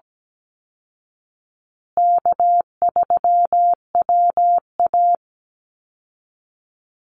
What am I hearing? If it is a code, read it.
Morse code, heard as a keyed tone, about 17 wpm, 700 Hz. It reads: K3WA
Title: Call Signs Repeated 3x - 17wpm